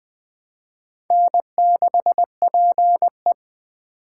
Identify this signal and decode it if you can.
Morse code, heard as a keyed tone, about 24 wpm, 700 Hz. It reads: N6PE
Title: Speed Racing - USA Call Signs 20wpm